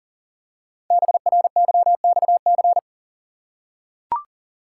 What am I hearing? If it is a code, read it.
Morse code, heard as a keyed tone, about 42 wpm, 700 Hz. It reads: BRYXC
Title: Code Groups - 5 Letters - 40wpm